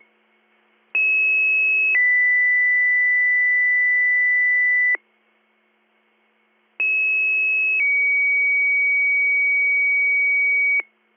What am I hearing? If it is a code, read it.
Two-tone page, heard as a one-second tone, then a three-second tone, about 2600 then 2000 Hz, about 2600 then 2250 Hz. It is heard from 0.9 to 10.8 s.